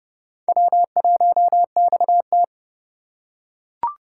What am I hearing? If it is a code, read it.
Morse code, heard as a keyed tone, about 28 wpm, 700 Hz. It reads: W1XT